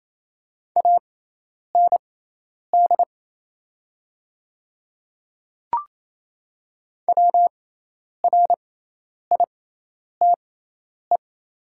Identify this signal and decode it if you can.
Morse code, heard as a keyed tone, about 28 wpm, spaced as for 10 wpm, 690 Hz. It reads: AND WRITE
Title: 6x Character Spacing - Top 200 Words - 28wpm